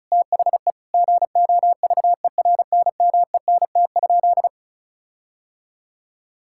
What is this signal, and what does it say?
Morse code, heard as a keyed tone, about 36 wpm, 700 Hz. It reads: THE GOVERNMENT?
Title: Sentences from Top 1600 Words 35wpm